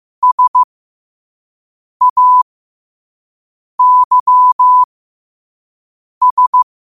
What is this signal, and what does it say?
Morse code, heard as a keyed tone, about 15 wpm, spaced as for 5 wpm, 1000 Hz. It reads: SAYS
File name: qotdfiles / 5wpm